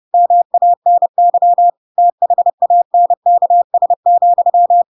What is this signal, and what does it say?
Morse code, heard as a keyed tone, about 30 wpm, 700 Hz. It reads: MANY THANKS,
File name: Pinocchio / Pinocchio ch5 30 30 ik1hge